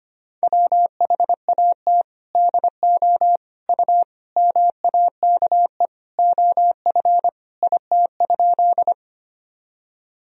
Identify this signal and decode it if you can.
Morse code, heard as a keyed tone, about 25 wpm, 700 Hz. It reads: WHAT DO U MAKE OF IT?
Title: Sentences from Top 100 Words and Abbreviations 25wpm